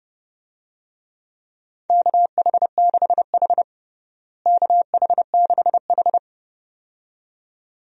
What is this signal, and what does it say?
Morse code, heard as a keyed tone, about 30 wpm, 700 Hz. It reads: KH6H KH6H
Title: DXCC-100 Call Signs - 4 Characters - Repeated 2x - 30wpm